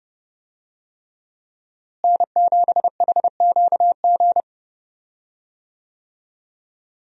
Morse code, heard as a keyed tone, about 30 wpm, 700 Hz. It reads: N7HQG
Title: USA Call Signs - 5 Characters - Repeated 3x - 30wpm